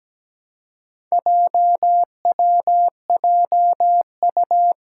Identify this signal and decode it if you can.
Morse code, heard as a keyed tone, about 17 wpm, 700 Hz. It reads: JWJU
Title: Four Challenging Letter-Number Combinations 17wpm